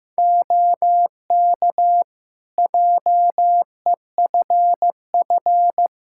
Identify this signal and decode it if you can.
Morse code, heard as a keyed tone, about 15 wpm, 700 Hz. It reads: OK JEFF
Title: Practice QSO - Exchange 4 - Ending - 15wpm